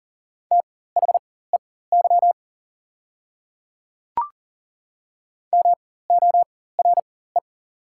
Morse code, heard as a keyed tone, about 40 wpm, spaced as for 19 wpm, 690 Hz. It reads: THEY MORE
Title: 4x Character Spacing - Top 100 Words - 40wpm